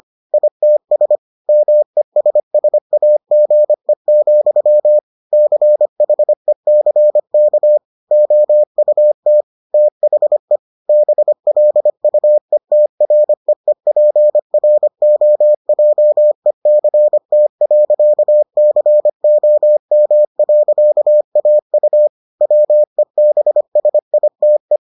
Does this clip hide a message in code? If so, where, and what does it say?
Yes: Morse code from 0.3 to 24.8 s, heard as a keyed tone, about 25 wpm, 600 Hz. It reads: ITS MESSAGE, CHECK OUT THE BLUETREEPROJECT.COM.AU WEBSITE